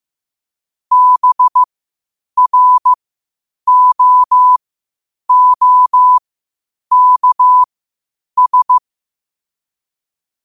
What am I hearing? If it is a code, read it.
Morse code, heard as a keyed tone, about 15 wpm, spaced as for 8 wpm, 1000 Hz. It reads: BROOKS
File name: qotdfiles / 8wpm